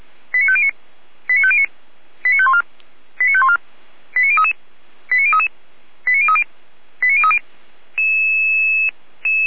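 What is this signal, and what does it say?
Five-tone selective call (ZVEI), heard as quick runs of five tones, tones of about 80 ms. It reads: 89509 89509 89514 89514 89033 89033 89030 89030
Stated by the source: ZVEI five-tone call received with an SDR by feuerwehr-orlamuende (fire brigade Orlamünde).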